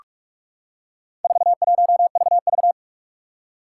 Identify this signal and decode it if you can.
Morse code, heard as a keyed tone, about 44 wpm, 700 Hz. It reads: 41VV
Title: Four Challenging Letter-Number Combinations 45wpm